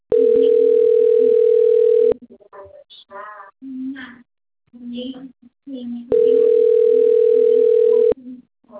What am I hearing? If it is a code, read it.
Ringback tone (North America), 440 + 480 Hz, with speech.
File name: out- - - - - .46444